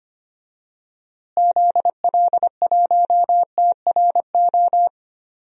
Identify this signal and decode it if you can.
Morse code, heard as a keyed tone, about 25 wpm, 700 Hz. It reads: ZL1TRO
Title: DXCC-100 Call Signs - 6 Characters - Repeated 2x - 25wpm